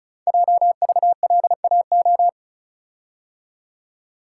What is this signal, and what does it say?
Morse code, heard as a keyed tone, about 36 wpm, 690 Hz. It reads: JVLAO
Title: Code Groups - 5 Letter:Numbers - 35wpm